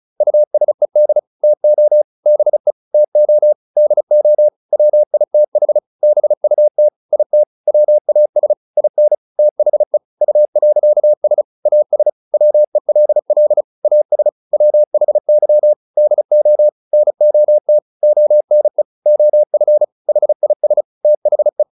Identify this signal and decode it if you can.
Morse code, heard as a keyed tone, about 34 wpm, 600 Hz. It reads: USED TO BE TO DO WITH BUT IT WAS IN THE U.S AS WELL AS WHY DO NOT ONE OF HIS THE